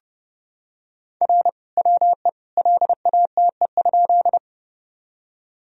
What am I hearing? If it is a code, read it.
Morse code, heard as a keyed tone, about 29 wpm, 700 Hz. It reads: R WE LATE?
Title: Sentences from Top 300 Words and Abbreviations 30wpm